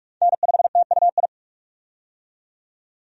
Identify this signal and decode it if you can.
Morse code, heard as a keyed tone, about 46 wpm, 700 Hz. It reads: N5TUI
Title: USA Call Signs - 5 Characters - 45wpm